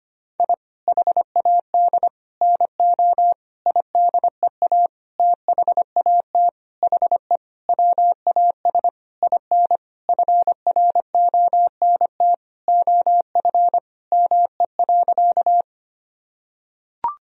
Morse code, heard as a keyed tone, about 25 wpm, 700 Hz. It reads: I HAD NO IDEA THAT HE WAS IN FRONT OF ME.
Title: Sentences from Top 1000 Words - Review Part 2 - 25wpm